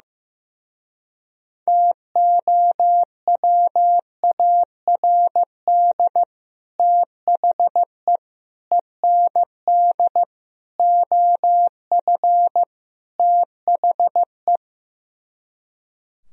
Morse code, heard as a keyed tone, about 15 wpm, 700 Hz. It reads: TOWARD THE END OF THE
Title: Speed-Racing - Sets of 5 Words 15wpm